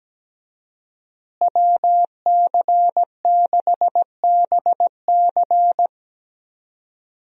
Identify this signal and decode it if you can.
Morse code, heard as a keyed tone, about 17 wpm, 700 Hz. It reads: WC6BC
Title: USA Call Signs - 5 Characters - Repeated 3x - 17wpm